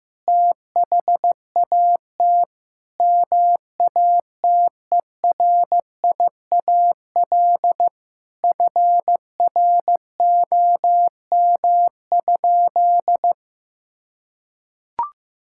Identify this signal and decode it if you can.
Morse code, heard as a keyed tone, about 15 wpm, 700 Hz. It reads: THAT MATERIAL FROM?
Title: Sentences from Top 1000 Words - Review Part 5 - 15wpm